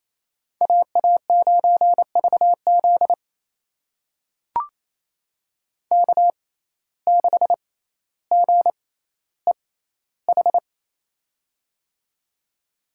Morse code, heard as a keyed tone, about 28 wpm, spaced as for 14 wpm, 700 Hz. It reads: AA9VZ K6GEH
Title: USA Call Signs - 5 Characters - 6x Character Spacing - 28wpm